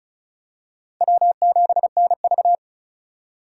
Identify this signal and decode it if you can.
Morse code, heard as a keyed tone, about 36 wpm, 700 Hz. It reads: W7NV
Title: USA Call Signs - 4 Characters - 2x Character Spacing - 35wpm